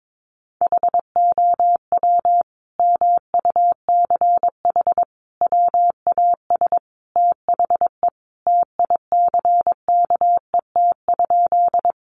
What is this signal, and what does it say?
Morse code, heard as a keyed tone, about 22 wpm, 700 Hz. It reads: HOW MUCH WAS THE TICKET?
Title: Sentences from Top 1400 Words 22wpm